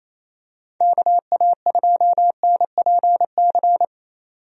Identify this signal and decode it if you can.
Morse code, heard as a keyed tone, about 27 wpm, 700 Hz. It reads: KA2NPC
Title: USA Call Signs - 6 Characters - 28wpm